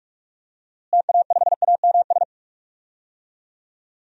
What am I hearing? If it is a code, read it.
Morse code, heard as a keyed tone, about 41 wpm, 700 Hz. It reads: TA5AMS
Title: DXCC-100 Call Signs - 6 Characters - Repeated 3x - 45wpm